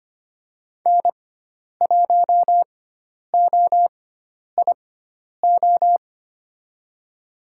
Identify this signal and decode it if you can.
Morse code, heard as a keyed tone, about 25 wpm, spaced as for 10 wpm, 700 Hz. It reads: N1OIO
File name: Lesson 16g - Letter c - Call Signs 25wpm